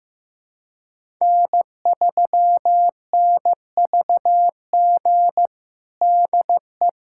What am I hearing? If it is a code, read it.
Morse code, heard as a keyed tone, about 15 wpm, 700 Hz. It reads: N3NVG DE
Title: USA Callsign DE Callsign - 5 Characters - 15wpm